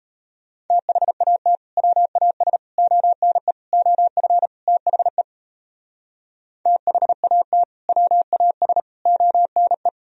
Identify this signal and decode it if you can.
Morse code, heard as a keyed tone, about 37 wpm, 700 Hz. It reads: THAT WAS ONE OF THE THAT WAS ONE